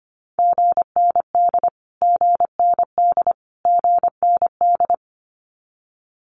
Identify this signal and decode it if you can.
Morse code, heard as a keyed tone, about 25 wpm, 700 Hz. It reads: GND GND GND
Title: Abbreviations Repeated 3x - 25wpm